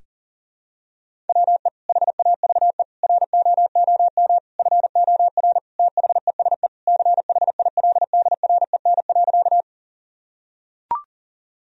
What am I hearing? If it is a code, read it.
Morse code, heard as a keyed tone, about 37 wpm, 700 Hz. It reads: WE HAVE ROOM FOR THESE CHILDREN.
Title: Sentences from Top 1000 Words - Review Part 1 - 40wpm